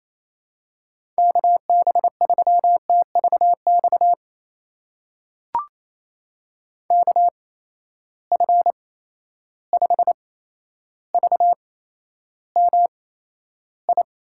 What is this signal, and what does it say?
Morse code, heard as a keyed tone, about 28 wpm, spaced as for 12 wpm, 700 Hz. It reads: KB3TVX KF5VMI
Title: USA Call Signs - 6 Characters - 8x Character Spacing - 28wpm